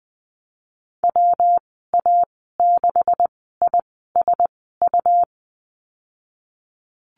Morse code, heard as a keyed tone, about 20 wpm, spaced as for 15 wpm, 700 Hz. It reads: WA6ISU
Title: 186 Lesson 30g - Letter b - Call Signs 20wpm